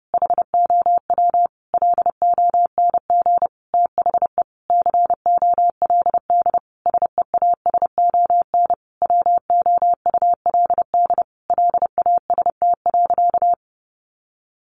Morse code, heard as a keyed tone, about 30 wpm, 700 Hz. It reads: HOW LONG THE COLD SEASON WOULD LAST.
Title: Sentences from Top 1400 Words 30wpm